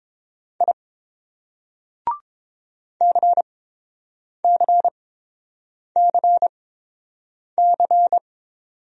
Morse code, heard as a keyed tone, about 31 wpm, spaced as for 7 wpm, 700 Hz. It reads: I CCCC